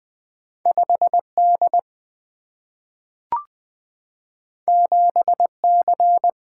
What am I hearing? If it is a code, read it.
Morse code, heard as a keyed tone, about 20 wpm, 700 Hz. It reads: 5D 7C
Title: Single-Letter Single-Number 20wpm